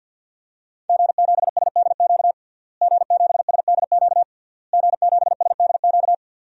Morse code, heard as a keyed tone, about 50 wpm, 700 Hz. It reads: G7SDQ G7SDQ G7SDQ
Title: DXCC-100 Call Signs - 5 Characters - Repeated 3x - 50wpm